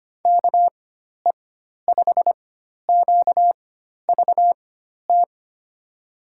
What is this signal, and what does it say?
Morse code, heard as a keyed tone, about 25 wpm, spaced as for 12 wpm, 700 Hz. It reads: KE5QVT
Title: USA Call Signs - 4x Character Spacing - 25wpm